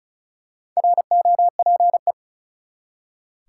Morse code, heard as a keyed tone, about 34 wpm, 700 Hz. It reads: ROPE
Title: Four Letter Words - 35wpm